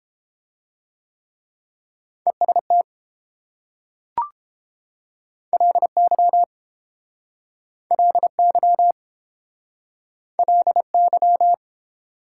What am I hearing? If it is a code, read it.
Morse code, heard as a keyed tone, about 30 wpm, 700 Hz. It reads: EST LY LY LY